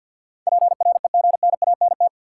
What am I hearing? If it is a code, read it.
Morse code, heard as a keyed tone, about 49 wpm, 700 Hz. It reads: PREGNANT
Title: Sentences from Top 1000 Words - Review Part 6 - New Word Drill - 50wpm